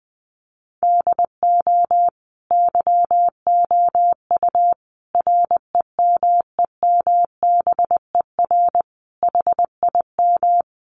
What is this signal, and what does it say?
Morse code, heard as a keyed tone, about 20 wpm, 700 Hz. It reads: DO YOU REMEMBER HIM
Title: Sentences from Top 1300 Words 20wpm